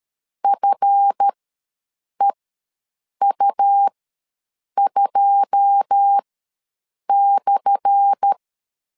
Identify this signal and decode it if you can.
Morse code, heard as a keyed tone, about 13 wpm, spaced as for 7 wpm, 800 Hz. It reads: FEU2/
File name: LEZIONI / 10 SIGNS / 10A SLASH ws 12wpm 3443